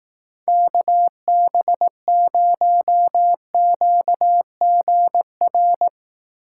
Morse code, heard as a keyed tone, about 18 wpm, 700 Hz. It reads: KB0QGR
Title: Speed Racing - USA Call Signs 15wpm